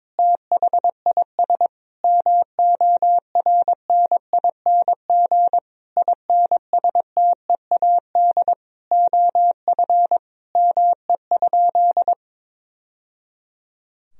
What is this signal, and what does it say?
Morse code, heard as a keyed tone, about 22 wpm, 700 Hz. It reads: THIS MORNING INSTEAD OF ME?
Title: Sentences from Top 1000 Words - Review Part 4 - 22wpm